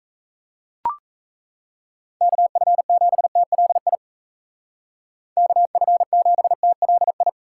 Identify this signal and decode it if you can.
Morse code, heard as a keyed tone, about 33 wpm, 700 Hz. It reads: KF7TLI KF7TLI
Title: Speed Racing - USA 2x3 Call Signs 28wpm